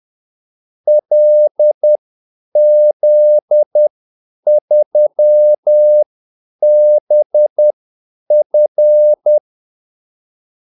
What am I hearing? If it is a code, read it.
Morse code, heard as a keyed tone, about 10 wpm, spaced as for 8 wpm, 600 Hz. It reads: LZ3BF
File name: Callsigns-10-8wpm